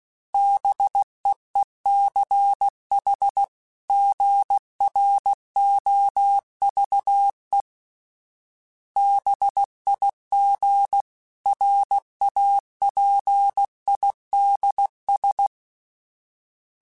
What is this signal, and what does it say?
Morse code, heard as a keyed tone, about 16 wpm, 800 Hz. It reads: BEECH GROVE BIG RAPIDS
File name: fromSMs / FromSMs mp3s / Cities@15